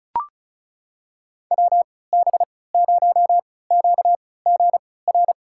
Morse code, heard as a keyed tone, about 36 wpm, spaced as for 20 wpm, 700 Hz. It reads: WB0QGR
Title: USA Call Signs - 3x Character Spacing - 35wpm